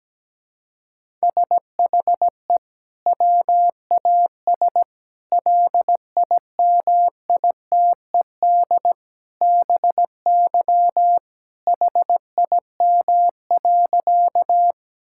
Morse code, heard as a keyed tone, about 17 wpm, 700 Hz. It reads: SHE WAS LIMITED BY HIM.
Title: Sentences from Top 1000 Words - Review Part 6 - 17wpm